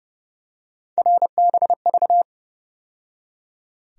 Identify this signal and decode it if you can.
Morse code, heard as a keyed tone, about 30 wpm, 700 Hz. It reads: RBV
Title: USA Call Sign Postfixes - 3 Letters 30wpm